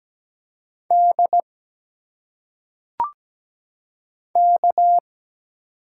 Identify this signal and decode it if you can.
Morse code, heard as a keyed tone, about 17 wpm, 700 Hz. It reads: D K
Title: Single Letters - Phonetic Pronunciation 17wpm